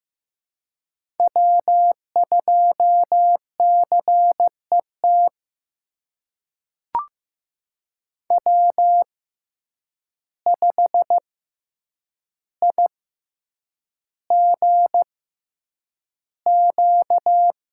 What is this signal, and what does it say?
Morse code, heard as a keyed tone, about 15 wpm, spaced as for 8 wpm, 700 Hz. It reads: W2CET W5IGQ
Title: USA Call Signs - 5 Characters - 6x Character Spacing - 15wpm